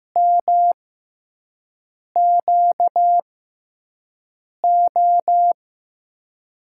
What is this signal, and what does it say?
Morse code, heard as a keyed tone, about 15 wpm, spaced as for 5 wpm, 700 Hz. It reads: MQO
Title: USA Call Signs - 6 Characters - 6x Character Spacing - 15wpm